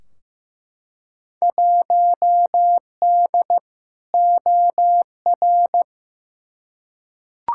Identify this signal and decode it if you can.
Morse code, heard as a keyed tone, about 15 wpm, 700 Hz. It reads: 1D OR
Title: Field Day Exchanges - 15wpm